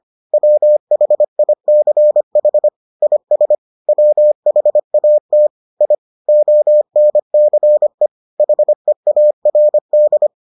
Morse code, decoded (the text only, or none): WHICH IS WHAT I ONCE HEARD